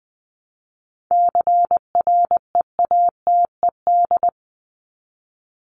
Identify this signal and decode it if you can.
Morse code, heard as a keyed tone, about 20 wpm, 700 Hz. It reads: CREATED
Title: ED Suffix - Top 100 Words with Base Words 20wpm